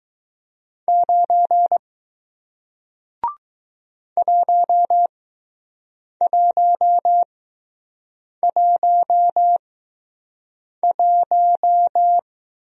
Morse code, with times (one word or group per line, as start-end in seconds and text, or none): 0.9-1.8 9
4.2-12.2 1111